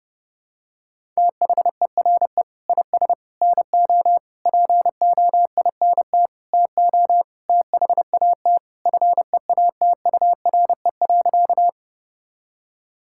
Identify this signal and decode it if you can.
Morse code, heard as a keyed tone, about 32 wpm, 700 Hz. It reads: THERE IS NO POINT TO THAT FEATURE.
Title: Sentences from Top 1000 Words - Review Part 5 - 30wpm